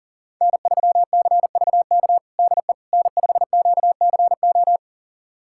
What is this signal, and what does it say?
Morse code, heard as a keyed tone, about 40 wpm, 700 Hz. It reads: N3CVK DE N5QCO
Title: USA Callsign DE Callsign - 5 Characters - 40wpm